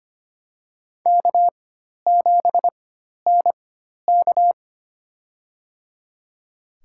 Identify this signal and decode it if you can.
Morse code, heard as a keyed tone, about 25 wpm, spaced as for 12 wpm, 700 Hz. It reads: K7NK
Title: USA Call Signs - 4 Characters - 4x Character Spacing - 25wpm